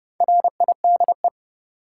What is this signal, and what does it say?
Morse code, heard as a keyed tone, about 32 wpm, 700 Hz. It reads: RIDE
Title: Sentences from Top 1200 Words - New Word Drill Alt - 30wpm